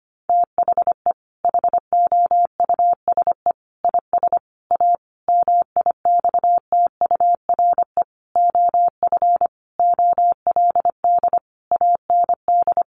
Morse code, heard as a keyed tone, about 25 wpm, 700 Hz. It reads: THE HOUSE IS A MIXTURE OF OLD AND
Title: Sentences from Top 1500 Words 25wpm